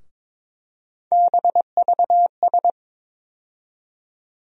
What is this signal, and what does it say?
Morse code, heard as a keyed tone, about 22 wpm, 700 Hz. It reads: BVS
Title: USA Call Sign Postfixes - 3 Letters 22wpm